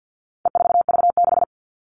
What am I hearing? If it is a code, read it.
Morse code, heard as a keyed tone, about 50 wpm, 700 Hz. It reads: E4V6
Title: All Cut Numbers - 4-Digit 50wpm